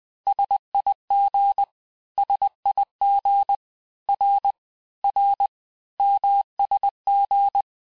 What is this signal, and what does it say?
Morse code, heard as a keyed tone, about 20 wpm, 800 Hz. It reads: SIG SIG R R MSG
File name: cours-du-lundi files / cours-1584-texte-abreviations-vitesse-19mpm-puis-24mpm